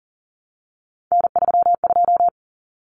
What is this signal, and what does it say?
Morse code, heard as a keyed tone, about 40 wpm, 700 Hz. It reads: N32
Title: Common Cut Numbers - 3-Digit 40wpm